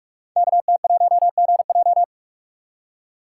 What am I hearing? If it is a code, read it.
Morse code, heard as a keyed tone, about 44 wpm, 690 Hz. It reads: KT1GJ
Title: Code Groups - 5 Letter:Numbers - 45wpm